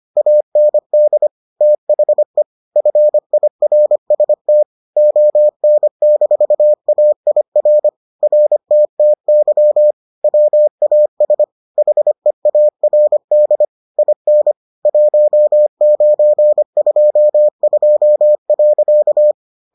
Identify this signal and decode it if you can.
Morse code, heard as a keyed tone, about 25 wpm, 600 Hz. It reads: AND THE FIRST ON-AIR RTTY WAS HEARD IN 1922.